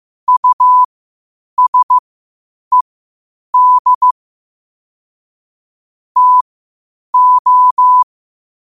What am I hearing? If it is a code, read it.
Morse code, heard as a keyed tone, about 15 wpm, spaced as for 8 wpm, 1000 Hz. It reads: USED TO
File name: qotdfiles / 8wpm